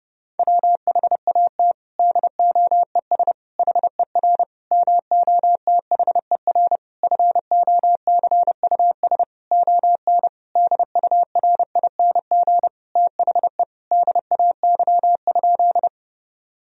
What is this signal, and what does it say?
Morse code, heard as a keyed tone, about 30 wpm, 700 Hz. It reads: WHAT DOES HER MOTHER FOCUS ON DURING THE DAY?
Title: Sentences from Top 1000 Words - Review Part 4 - 30wpm